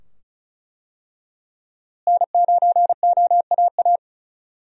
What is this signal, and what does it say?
Morse code, heard as a keyed tone, about 34 wpm, 700 Hz. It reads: N9OAA
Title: USA Call Signs - 5 Characters - Repeated 2x - 35wpm